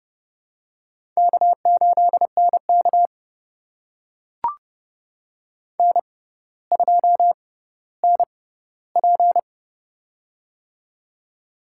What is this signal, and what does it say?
Morse code, heard as a keyed tone, about 29 wpm, spaced as for 15 wpm, 700 Hz. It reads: K8NK N2NP
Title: USA Call Signs - 4 Characters - 6x Character Spacing - 30wpm